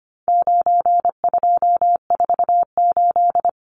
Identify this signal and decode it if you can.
Morse code, heard as a keyed tone, about 25 wpm, 700 Hz. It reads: 9248
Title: Numbers - 4-Digit 25wpm